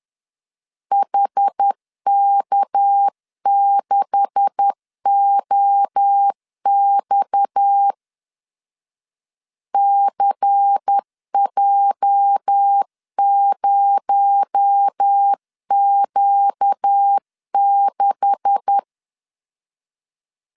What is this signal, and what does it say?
Morse code, heard as a keyed tone, about 10 wpm, 800 Hz. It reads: HK6OX CJ0Q6
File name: LEZIONI / 9 286 10wpm 2735